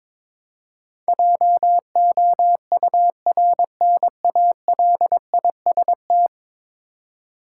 Morse code, heard as a keyed tone, about 22 wpm, 700 Hz. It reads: JOURNALIST
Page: morse-code-ninja-practice-22wpm